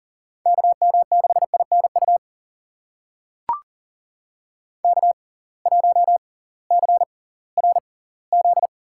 Morse code, heard as a keyed tone, about 39 wpm, spaced as for 21 wpm, 700 Hz. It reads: KM6INU K1CRZ
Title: USA Call Signs - 6x Character Spacing - 40wpm